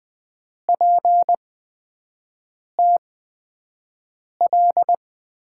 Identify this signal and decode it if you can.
Morse code, heard as a keyed tone, about 20 wpm, spaced as for 5 wpm, 700 Hz. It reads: PTL
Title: USA Call Signs - 5 Characters - 8x Character Spacing - 20wpm